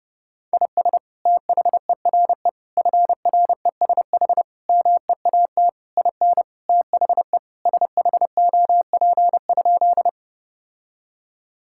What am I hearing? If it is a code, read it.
Morse code, heard as a keyed tone, about 32 wpm, 700 Hz. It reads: IS THERE FRESH MEAT IN THE SHOP?
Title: Sentences from Top 1600 Words 30wpm